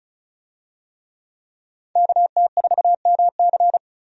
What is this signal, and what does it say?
Morse code, heard as a keyed tone, about 34 wpm, 700 Hz. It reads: KT4MC
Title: USA Call Signs - 5 Characters - 35wpm